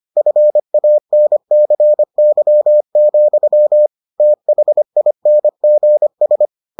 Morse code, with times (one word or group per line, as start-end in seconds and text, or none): 0.2-3.9 FANCY,
4.2-6.5 THINGS